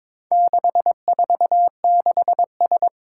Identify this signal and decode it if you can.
Morse code, heard as a keyed tone, about 22 wpm, 700 Hz. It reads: 646S
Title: Four Challenging Letter-Number Combinations 22wpm